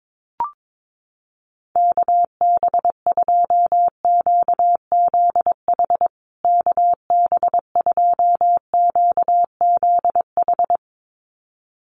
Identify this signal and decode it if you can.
Morse code, heard as a keyed tone, about 22 wpm, 700 Hz. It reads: KB2QZH KB2QZH
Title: USA Call Signs Repeated 2x 22wpm